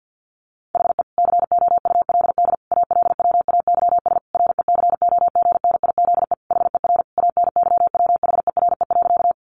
Morse code, heard as a keyed tone, about 50 wpm, 700 Hz. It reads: HE COULD ALWAYS RECOGNIZE HER ANYWHERE.
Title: Sentences from Top 1400 Words 50wpm